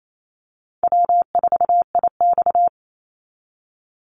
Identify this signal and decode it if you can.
Morse code, heard as a keyed tone, about 28 wpm, 700 Hz. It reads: W4IX
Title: Easy Call Signs Repeated 2x - 28wpm